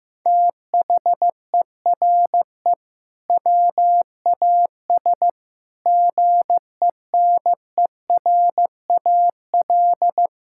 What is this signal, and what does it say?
Morse code, heard as a keyed tone, about 16 wpm, 700 Hz. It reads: THERE WAS GENERAL
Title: Sentences from Top 1600 Words 15wpm